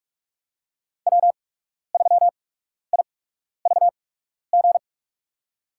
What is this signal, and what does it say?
Morse code, heard as a keyed tone, about 42 wpm, spaced as for 12 wpm, 700 Hz. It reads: W3IVG
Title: USA Call Signs - 5 Characters - 8x Character Spacing - 45wpm